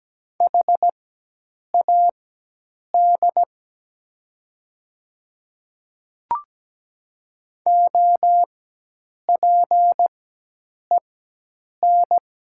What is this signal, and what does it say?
Morse code, heard as a keyed tone, about 17 wpm, spaced as for 8 wpm, 700 Hz. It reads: HAD OPEN
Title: 4x Character Spacing - Top 300 Words - 17wpm